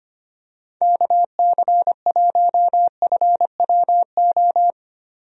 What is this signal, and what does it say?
Morse code, heard as a keyed tone, about 25 wpm, 700 Hz. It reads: KC1FWO
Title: USA Call Signs - 6 Characters - Repeated 3x - 25wpm